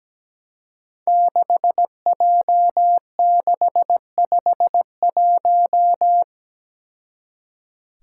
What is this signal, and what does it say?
Morse code, heard as a keyed tone, about 17 wpm, 700 Hz. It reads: 6J651
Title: Five Challenging Letter-Number Combinations 17wpm